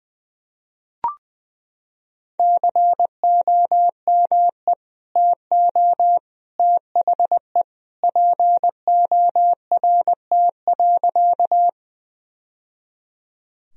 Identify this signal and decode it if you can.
Morse code, heard as a keyed tone, about 20 wpm, 700 Hz. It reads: COME TO THE PORT.
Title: Sentences from Top 1000 Words - Review Part 1 - 20wpm